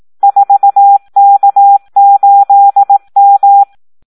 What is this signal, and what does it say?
Morse code, heard as a keyed tone, about 18 wpm, 800 Hz. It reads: 4K8M
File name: Audio practica CW / SD 1 18